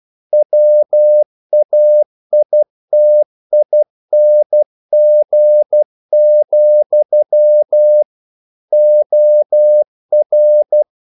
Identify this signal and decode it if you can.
Morse code, heard as a keyed tone, about 12 wpm, 600 Hz. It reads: WAITING, OR
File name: CW / Practice / If12